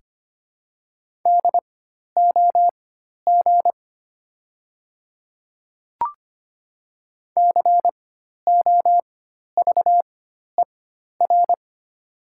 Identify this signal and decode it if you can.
Morse code, heard as a keyed tone, about 25 wpm, spaced as for 12 wpm, 690 Hz. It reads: DOG COVER
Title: 4x Character Spacing - Top 500 Words - 25wpm